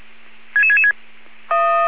Five-tone selective call (ZVEI), heard as quick runs of five tones, tones of about 70 ms. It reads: 66666